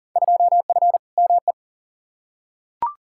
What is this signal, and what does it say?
Morse code, heard as a keyed tone, about 40 wpm, 700 Hz. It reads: 2F ME
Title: Field Day Exchanges - 40wpm